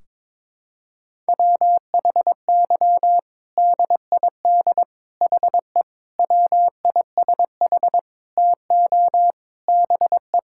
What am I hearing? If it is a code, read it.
Morse code, heard as a keyed tone, about 22 wpm, 700 Hz. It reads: WHY DID HE WISH TO BE
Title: Sentences from Top 1600 Words 22wpm